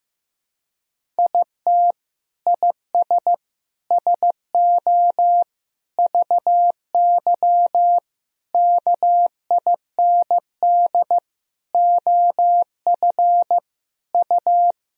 Morse code, heard as a keyed tone, about 15 wpm, 700 Hz. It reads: IT IS SO VY KIND OF U
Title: Sentences from Top 300 Words and Abbreviations 15wpm